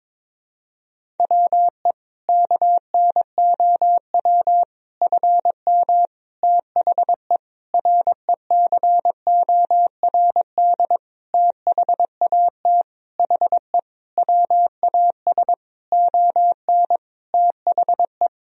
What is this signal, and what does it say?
Morse code, heard as a keyed tone, about 22 wpm, 700 Hz. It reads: WE KNOW FM THE RECORD THAT HE WAS ON THE
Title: Sentences from Top 500 Words and Abbreviations 22wpm